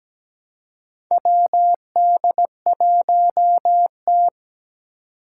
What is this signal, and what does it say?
Morse code, heard as a keyed tone, about 17 wpm, 700 Hz. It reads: WD1T
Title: Four Challenging Letter-Number Combinations 17wpm